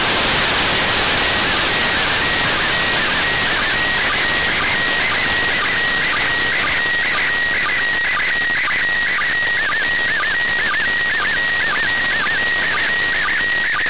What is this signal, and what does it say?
SSTV (slow-scan television), PD 120.
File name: eb / ISS-SSTV-audio z 02-31-47 14-11-2024